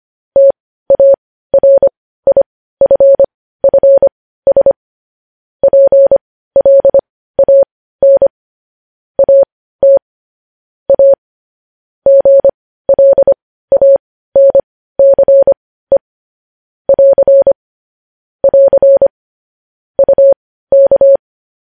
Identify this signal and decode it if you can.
Morse code, heard as a keyed tone, about 25 wpm, spaced as for 15 wpm, 550 Hz. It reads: TARIFFS PLAN AT A GLANCE <AR> <AR> UK
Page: news headlines in morse code at 15 wpm